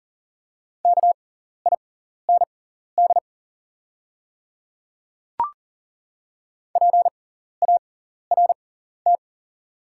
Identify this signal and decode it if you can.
Morse code, heard as a keyed tone, about 40 wpm, spaced as for 14 wpm, 700 Hz. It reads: KIND PART
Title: 6x Character Spacing - Top 200 Words - 40wpm